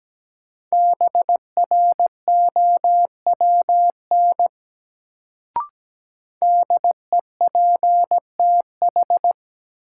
Morse code, heard as a keyed tone, about 17 wpm, 700 Hz. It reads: BROWN DEPTH
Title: Sentences from Top 1200 Words - New Word Drill - 17wpm